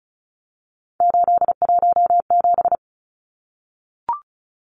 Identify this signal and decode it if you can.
Morse code, heard as a keyed tone, about 34 wpm, 700 Hz. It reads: 817
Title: Numbers - 3-Digit 35wpm